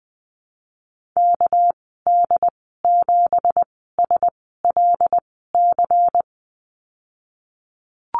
Morse code, heard as a keyed tone, about 20 wpm, spaced as for 15 wpm, 700 Hz. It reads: KD7SLC
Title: 231 Lesson 37g - Letter x - Call Signs 20wpm